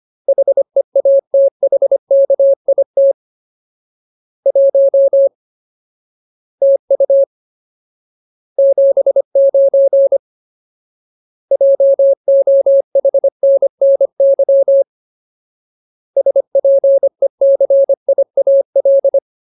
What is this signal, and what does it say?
Morse code, heard as a keyed tone, about 25 wpm, 550 Hz. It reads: HEATHKIT 1 TU 79 JOHNNY SPECIAL
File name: Ultimate Word List / 25WPM Ultimate